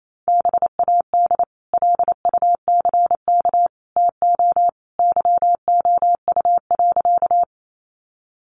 Morse code, heard as a keyed tone, about 28 wpm, 700 Hz. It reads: BAD LUCK TO YOU.
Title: Sentences from Top 1500 Words 28wpm